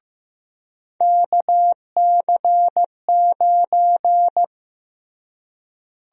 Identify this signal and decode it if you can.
Morse code, heard as a keyed tone, about 15 wpm, 700 Hz. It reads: KC9
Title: USA Call Sign Prefixes - Repeated 2x - 15wpm